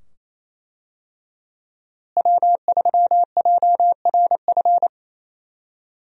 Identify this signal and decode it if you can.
Morse code, heard as a keyed tone, about 28 wpm, 700 Hz. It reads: W3JRF
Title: USA Call Signs - 5 Characters - Repeated 2x - 28wpm